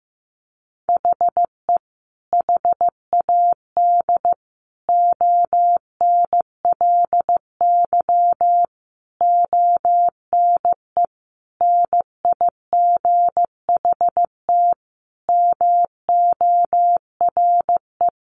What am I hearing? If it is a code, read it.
Morse code, heard as a keyed tone, about 15 wpm, 700 Hz. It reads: HE HAD ONLY ONE NIGHT MORE
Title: Sentences from Top 300 Words 15wpm